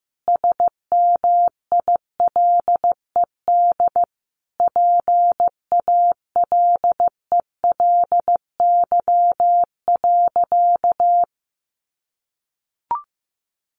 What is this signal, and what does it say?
Morse code, heard as a keyed tone, about 15 wpm, 700 Hz. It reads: SMILED PALELY.
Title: A Sound Of Thunder 15wpm